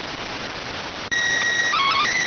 SSTV (slow-scan television), PD 290.